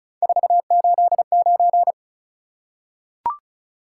Morse code, heard as a keyed tone, about 36 wpm, 700 Hz. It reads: 489
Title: Top 180 QSO Elements 35wpm